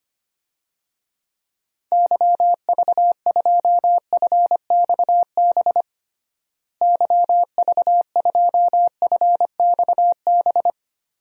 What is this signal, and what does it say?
Morse code, heard as a keyed tone, about 25 wpm, 700 Hz. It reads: YV2FXB YV2FXB
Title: DXCC-100 Call Signs - 6 Characters - Repeated 3x - 25wpm